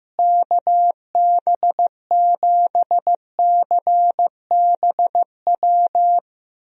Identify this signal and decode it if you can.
Morse code, heard as a keyed tone, about 15 wpm, 700 Hz. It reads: KB7CBW
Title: USA Call Signs - 6 Characters - Repeated 3x - 15wpm